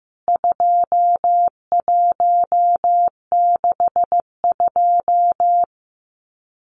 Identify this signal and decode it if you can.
Morse code, heard as a keyed tone, about 15 wpm, 700 Hz. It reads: 2162
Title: CWT Exchange 15wpm